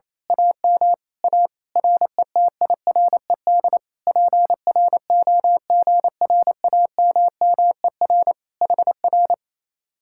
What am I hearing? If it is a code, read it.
Morse code, heard as a keyed tone, about 27 wpm, 700 Hz. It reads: AM A RETIRED PROGRAMMER HR
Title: Speed-Racing Common QSO Phrases Encore - 28wpm